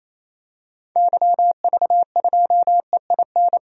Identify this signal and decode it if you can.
Morse code, heard as a keyed tone, about 27 wpm, 700 Hz. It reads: YV2EIN
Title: DXCC-100 Call Signs - 6 Characters - 28wpm